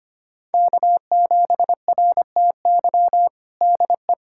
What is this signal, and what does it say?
Morse code, heard as a keyed tone, about 25 wpm, 700 Hz. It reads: K7RTY DE
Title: USA Callsign DE Callsign 25wpm